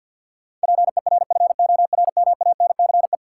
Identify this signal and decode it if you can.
Morse code, heard as a keyed tone, about 52 wpm, 700 Hz. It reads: PERFORMANCE
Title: Sentences from Top 1000 Words - New Word Drill - 50wpm